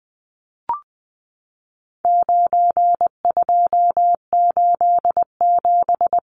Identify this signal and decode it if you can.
Morse code, heard as a keyed tone, about 20 wpm, 700 Hz. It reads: 9287
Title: Numbers - 4-Digit 20wpm